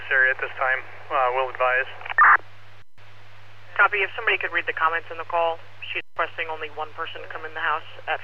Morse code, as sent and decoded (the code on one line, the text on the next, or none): none